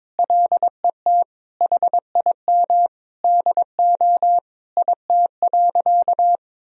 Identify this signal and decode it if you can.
Morse code, heard as a keyed tone, about 22 wpm, 700 Hz. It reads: LET HIM DO IT.
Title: Sentences from Top 1000 Words - Review Part 3 - 22wpm